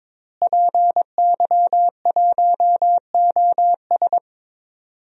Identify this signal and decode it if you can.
Morse code, heard as a keyed tone, about 22 wpm, 700 Hz. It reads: PY1OS
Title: DXCC-100 Call Signs - 5 Characters - Repeated 3x - 22wpm